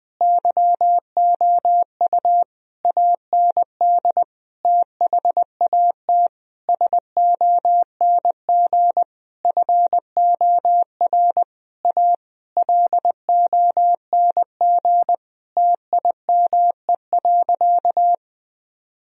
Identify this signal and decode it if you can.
Morse code, heard as a keyed tone, about 21 wpm, 700 Hz. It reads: YOU AND THAT SONG FOR A LONG TIME.
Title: Sentences from Top 1000 Words - Review Part 2 - 20wpm